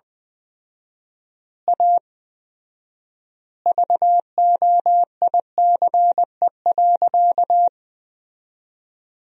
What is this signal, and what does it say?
Morse code, heard as a keyed tone, about 20 wpm, 700 Hz. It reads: A VOICE.
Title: 4x Word-Spacing - Sentences from Top 500 Words - 20wpm